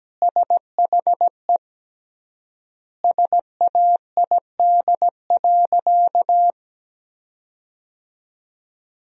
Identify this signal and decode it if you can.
Morse code, heard as a keyed tone, about 17 wpm, 700 Hz. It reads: SHE SAID.
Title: 3x Word-Spacing - Sentences from Top 400 Words - 17wpm